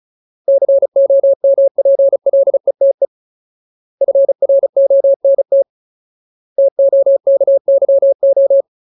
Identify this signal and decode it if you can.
Morse code, heard as a keyed tone, about 34 wpm, 550 Hz. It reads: COMPLETE FRONT TOKYO
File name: Ultimate Word List / 35WPM Ultimate